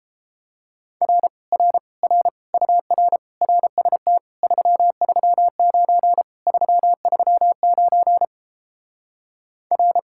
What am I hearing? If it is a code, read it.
Morse code, heard as a keyed tone, about 31 wpm, 700 Hz. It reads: R R R UR RST 339 339 R